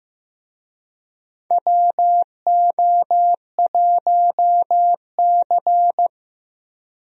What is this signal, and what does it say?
Morse code, heard as a keyed tone, about 16 wpm, 700 Hz. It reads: WO1C
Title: USA Call Signs - 4 Characters - Repeated 3x - 15wpm